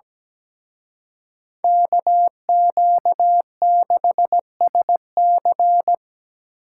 Morse code, heard as a keyed tone, about 20 wpm, 700 Hz. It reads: KQ6SC
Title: Speed Racing - USA 2x2 Call Signs 17wpm